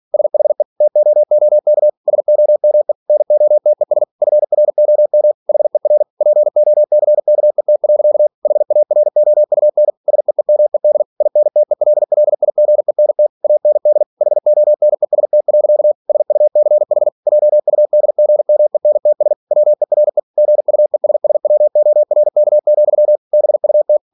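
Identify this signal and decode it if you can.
Morse code, heard as a keyed tone, about 46 wpm, 600 Hz. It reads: SHE TOOK SOME NOTES FROM HER POCKET. HAROUN SEEMED INTELLIGENT AND HONEST. SUCH JUDGMENTS WERE GUESSWORK, BUT
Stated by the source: Practice text from an English novel sent at 47 WPM.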